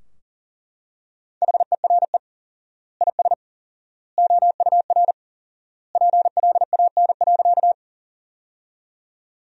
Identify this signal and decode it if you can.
Morse code, heard as a keyed tone, about 37 wpm, 700 Hz. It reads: HERE IS OUR PLAN.
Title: 4x Word-Spacing - Sentences from Top 500 Words - 40wpm